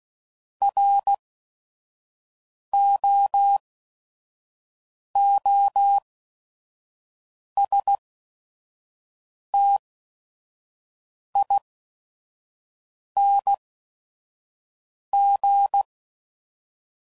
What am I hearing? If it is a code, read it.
Morse code, heard as a keyed tone, about 16 wpm, spaced as for 5 wpm, 800 Hz. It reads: ROOSTING
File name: Prefixes / By First Letter / Rs